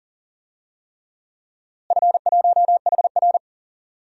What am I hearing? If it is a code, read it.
Morse code, heard as a keyed tone, about 40 wpm, 700 Hz. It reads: F1HR
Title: DXCC-100 Call Signs - 4 Characters - Repeated 3x - 40wpm